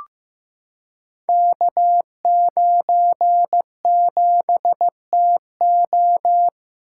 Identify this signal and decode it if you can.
Morse code, heard as a keyed tone, about 15 wpm, 700 Hz. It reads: K97TO
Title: Code Groups - 5 Letter:Numbers - 15wpm